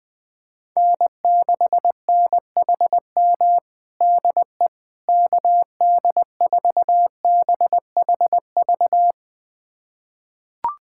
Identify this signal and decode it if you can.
Morse code, heard as a keyed tone, about 20 wpm, 700 Hz. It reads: N6NHM DE KD4BHV
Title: USA Callsign DE Callsign 20wpm